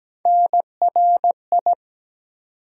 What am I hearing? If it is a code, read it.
Morse code, heard as a keyed tone, about 17 wpm, 700 Hz. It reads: NRI
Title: USA Call Sign Postfixes - 3 Letters 17wpm